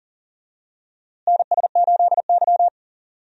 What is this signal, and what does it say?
Morse code, heard as a keyed tone, about 38 wpm, 700 Hz. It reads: NS8Y
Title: USA Call Signs - 4 Characters - Repeated 2x - 40wpm